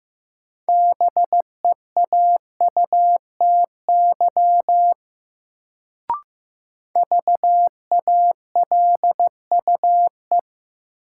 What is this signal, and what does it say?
Morse code, heard as a keyed tone, about 15 wpm, 700 Hz. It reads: BEAUTY VALUE
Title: Sentences from Top 600 Words - New Word Drill - 15wpm